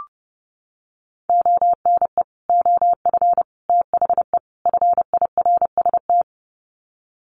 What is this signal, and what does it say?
Morse code, heard as a keyed tone, about 28 wpm, 700 Hz. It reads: ONE OF THE FIRST
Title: Sets of 4 Words 30wpm